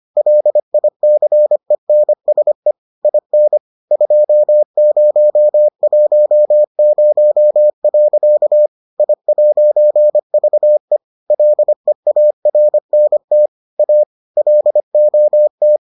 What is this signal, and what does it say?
Morse code, heard as a keyed tone, about 25 wpm, 600 Hz. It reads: LICENSE IN 2010. I'VE LEARNT A LOT